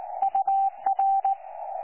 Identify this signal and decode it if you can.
Morse code, heard as a keyed tone, about 17 wpm, 780 Hz. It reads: UR
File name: 12 39Z 18088.2kHz utah aa7fv k9tjl